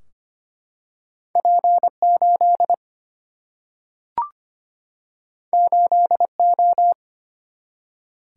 Morse code, heard as a keyed tone, about 25 wpm, 700 Hz. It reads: P 8 8 O
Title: Single-Letter Single-Number 25wpm